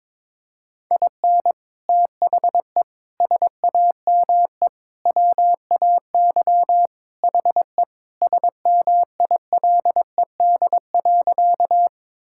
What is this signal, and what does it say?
Morse code, heard as a keyed tone, about 22 wpm, 700 Hz. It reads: IN THE SAME WAY HE SMILED.
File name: practice / 22wpm / Sentences with 4-Word Phrases - 22wpm